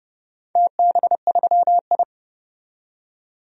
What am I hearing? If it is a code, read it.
Morse code, heard as a keyed tone, about 30 wpm, 700 Hz. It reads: TB3I
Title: DXCC-100 Call Signs - 4 Characters - 30wpm